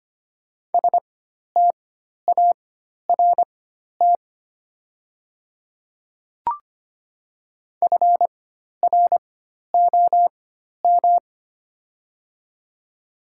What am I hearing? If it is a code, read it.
Morse code, heard as a keyed tone, about 25 wpm, spaced as for 12 wpm, 690 Hz. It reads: START FROM
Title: 4x Character Spacing - Top 500 Words - 25wpm